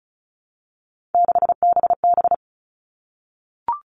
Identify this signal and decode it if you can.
Morse code, heard as a keyed tone, about 34 wpm, 700 Hz. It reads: 6BB
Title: All Cut Numbers - 3-Digit 35wpm